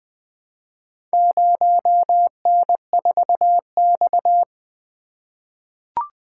Morse code, heard as a keyed tone, about 20 wpm, 700 Hz. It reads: 0N4X
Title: Four Letter-Number Combinations - 20wpm